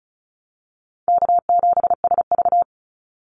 Morse code, heard as a keyed tone, about 36 wpm, 700 Hz. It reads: K7SV
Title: Call Signs - 35wpm